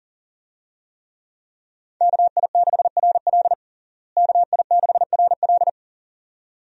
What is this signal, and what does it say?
Morse code, heard as a keyed tone, about 40 wpm, 700 Hz. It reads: KI6RL KI6RL
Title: USA Call Signs - 5 Characters - Repeated 2x - 40wpm